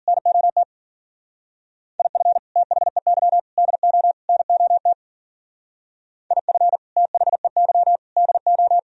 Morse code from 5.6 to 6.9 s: IF

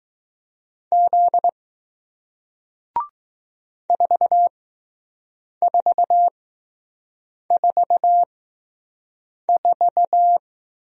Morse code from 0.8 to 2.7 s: Z

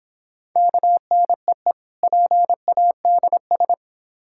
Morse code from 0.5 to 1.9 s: KNEE

